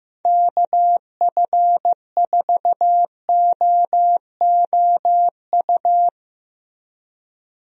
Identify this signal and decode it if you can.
Morse code, heard as a keyed tone, about 16 wpm, 700 Hz. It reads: KF4OOU
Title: USA Call Signs - 6 Characters - Repeated 2x - 15wpm